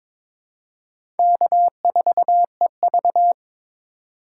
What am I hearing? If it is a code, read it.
Morse code, heard as a keyed tone, about 22 wpm, 700 Hz. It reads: K4EV
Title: USA Call Signs - 3x Character Spacing - 22wpm